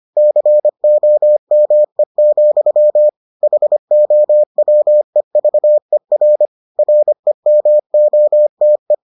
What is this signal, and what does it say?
Morse code, heard as a keyed tone, about 25 wpm, 600 Hz. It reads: COME, HOWEVER REMOTE